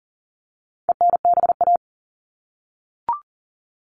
Morse code, heard as a keyed tone, about 40 wpm, 700 Hz. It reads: ENBA
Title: All Cut Numbers - 4-Digit 40wpm